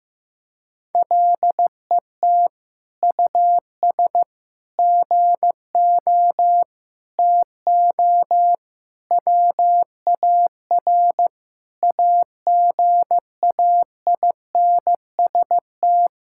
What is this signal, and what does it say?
Morse code, heard as a keyed tone, about 15 wpm, 700 Hz. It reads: LET US GO TO WAR AGAINST
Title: Sentences from Top 1000 Words - Review Part 5 - 15wpm